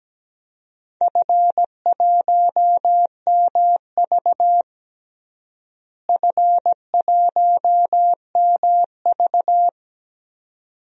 Morse code, heard as a keyed tone, about 17 wpm, 700 Hz. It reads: F1MV F1MV
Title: DXCC-100 Call Signs - 4 Characters - Repeated 2x - 17wpm